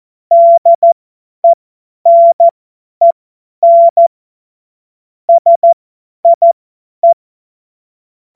Morse code, heard as a keyed tone, about 14 wpm, spaced as for 10 wpm, 680 Hz. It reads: DENEN SIE
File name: Morsekurs / Witze / Witz 30. November 2021, 50 F70